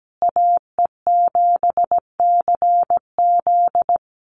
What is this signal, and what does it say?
Morse code, heard as a keyed tone, about 17 wpm, 700 Hz. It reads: AE7CZ
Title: USA Call Signs Repeated 3x 17wpm